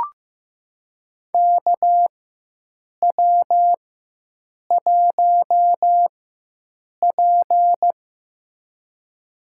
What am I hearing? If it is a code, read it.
Morse code, heard as a keyed tone, about 15 wpm, spaced as for 7 wpm, 700 Hz. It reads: KW1P
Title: USA Call Signs - 4 Characters - 4x Character Spacing - 15wpm